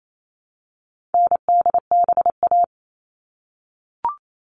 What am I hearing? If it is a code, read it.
Morse code, heard as a keyed tone, about 28 wpm, 700 Hz. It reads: NDBA
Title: All Cut Numbers - 4-Digit 28wpm